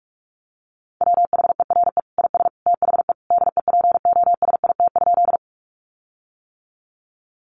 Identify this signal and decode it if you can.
Morse code, heard as a keyed tone, about 44 wpm, 700 Hz. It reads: WHERE IS THE DEPOSIT?
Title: Sentences from Top 1300 Words 45wpm